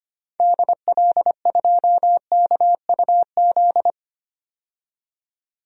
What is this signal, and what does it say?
Morse code, heard as a keyed tone, about 25 wpm, 700 Hz. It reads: DL2KUZ
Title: DXCC-100 Call Signs - 6 Characters - 25wpm